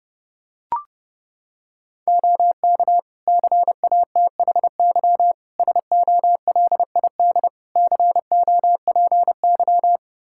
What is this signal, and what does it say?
Morse code, heard as a keyed tone, about 32 wpm, 700 Hz. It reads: OK CATHY SOLID COPY
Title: K1USN SST QSO Practice 30wpm